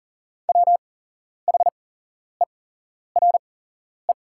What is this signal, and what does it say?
Morse code, heard as a keyed tone, about 40 wpm, spaced as for 11 wpm, 700 Hz. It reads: WHERE
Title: 8x Character Spacing - Top 200 Words - 40wpm